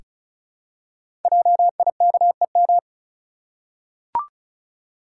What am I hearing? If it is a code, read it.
Morse code, heard as a keyed tone, about 36 wpm, 690 Hz. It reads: JIKEM
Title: Code Groups - 5 Letters - 35wpm